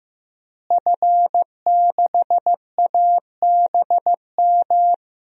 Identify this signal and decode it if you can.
Morse code, heard as a keyed tone, about 15 wpm, 700 Hz. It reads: F6ABM
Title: DXCC-100 Call Signs - 5 Characters - Repeated 3x - 15wpm